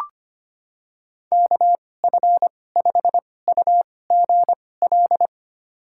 Morse code, heard as a keyed tone, about 25 wpm, spaced as for 18 wpm, 690 Hz. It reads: KF5UGL
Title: USA Call Signs - 6 Characters - 2x Character Spacing - 25wpm